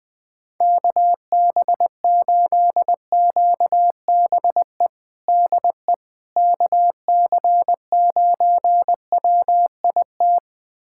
Morse code, heard as a keyed tone, about 21 wpm, 700 Hz. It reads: KB8QBE DE KC9WIT
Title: USA Callsign DE Callsign 20wpm